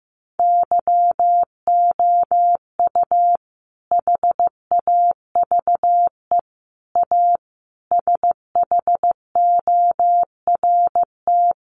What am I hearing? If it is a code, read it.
Morse code, heard as a keyed tone, about 14 wpm, 700 Hz. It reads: YOU HAVE A SHORT
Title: Sentences from Top 1300 Words 15wpm